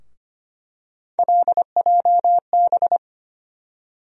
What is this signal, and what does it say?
Morse code, heard as a keyed tone, about 25 wpm, 700 Hz. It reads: LJB
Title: USA Call Sign Postfixes - 3 Letters 25wpm